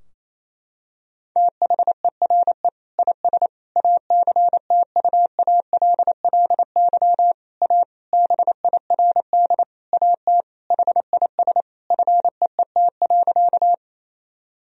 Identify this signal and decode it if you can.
Morse code, heard as a keyed tone, about 28 wpm, 700 Hz. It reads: THERE IS ACTUALLY A BIRD AT HIS FEET.
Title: Sentences from Top 1000 Words - Review Part 4 - 28wpm